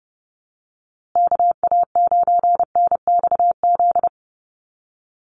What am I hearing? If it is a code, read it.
Morse code, heard as a keyed tone, about 30 wpm, 700 Hz. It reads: KA9NXZ
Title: USA Call Signs Repeated 2x 30wpm